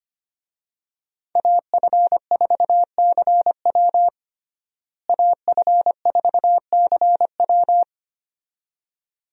Morse code, heard as a keyed tone, about 25 wpm, 700 Hz. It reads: AF4CW AF4CW
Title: USA Call Signs - 5 Characters - Repeated 2x - 25wpm